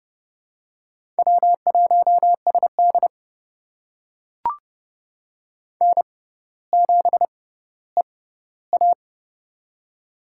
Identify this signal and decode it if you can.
Morse code, heard as a keyed tone, about 28 wpm, spaced as for 15 wpm, 700 Hz. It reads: W1SD N7EA
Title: USA Call Signs - 4 Characters - 6x Character Spacing - 30wpm